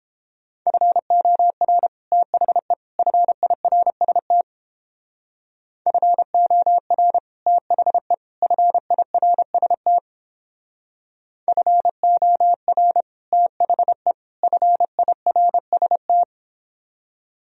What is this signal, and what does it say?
Morse code, heard as a keyed tone, about 29 wpm, 700 Hz. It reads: FOR THE FIRST FOR THE FIRST FOR THE FIRST